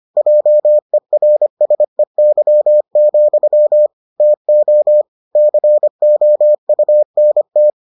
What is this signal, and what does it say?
Morse code, heard as a keyed tone, about 25 wpm, 600 Hz. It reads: JERSEY, TO COUNT